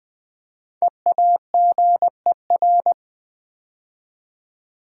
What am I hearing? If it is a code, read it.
Morse code, heard as a keyed tone, about 20 wpm, 690 Hz. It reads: EAGER
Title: 5 Letter Words - First 3 Letters Given - 20wpm